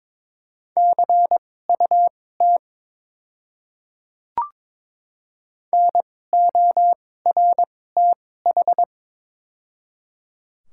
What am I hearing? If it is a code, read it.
Morse code, heard as a keyed tone, about 22 wpm, spaced as for 16 wpm, 700 Hz. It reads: CUT NORTH
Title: 2x Character Spacing - Top 500 Words - 22wpm